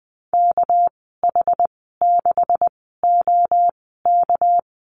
Morse code, heard as a keyed tone, about 20 wpm, spaced as for 15 wpm, 700 Hz. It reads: KH6OK
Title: 219 Lesson 35g - Number 0 - Call Signs 20wpm